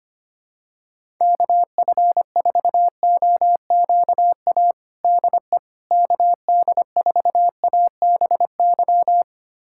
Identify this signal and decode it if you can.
Morse code, heard as a keyed tone, about 25 wpm, 700 Hz. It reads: KF4OQA DE KD4ABY
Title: USA Callsign DE Callsign 25wpm